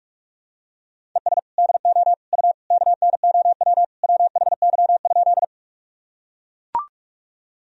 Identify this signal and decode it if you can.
Morse code, heard as a keyed tone, about 43 wpm, 700 Hz. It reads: ES DO U KNOW WHY?
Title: Sentences from Top 200 Words and Abbreviations 45wpm